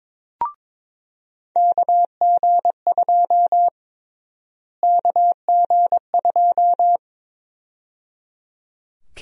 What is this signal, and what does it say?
Morse code, heard as a keyed tone, about 22 wpm, 700 Hz. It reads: KG2 KG2
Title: USA Call Sign Prefixes - Repeated 2x - 22wpm